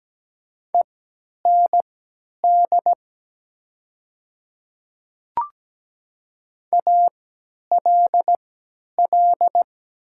Morse code, heard as a keyed tone, about 17 wpm, spaced as for 10 wpm, 700 Hz. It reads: END ALL
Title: 3x Character Spacing - Top 200 Words - 17wpm